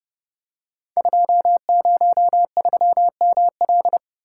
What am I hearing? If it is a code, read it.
Morse code, heard as a keyed tone, about 28 wpm, 700 Hz. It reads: 203ML
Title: Code Groups - 5 Letter:Numbers - 30wpm